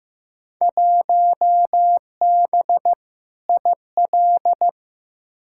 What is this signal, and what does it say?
Morse code, heard as a keyed tone, about 16 wpm, 700 Hz. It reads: 1B IL
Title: Field Day Exchanges - 15wpm